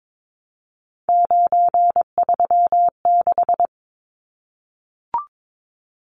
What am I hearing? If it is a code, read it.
Morse code, heard as a keyed tone, about 22 wpm, 700 Hz. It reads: 936
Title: Numbers - 3-Digit 22wpm